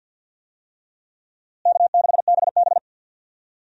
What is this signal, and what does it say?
Morse code, heard as a keyed tone, about 52 wpm, 700 Hz. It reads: K6BB
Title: USA Call Signs - 4 Characters - Repeated 3x - 50wpm